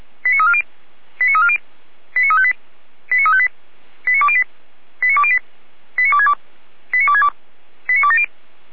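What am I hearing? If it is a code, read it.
Five-tone selective call (ZVEI), heard as quick runs of five tones, tones of about 70 ms. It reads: 89340 89340 89358 89358 89208 89208 89262 89262 89270
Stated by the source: ZVEI five-tone call received by SDR, archived by Feuerwehr Orlamünde.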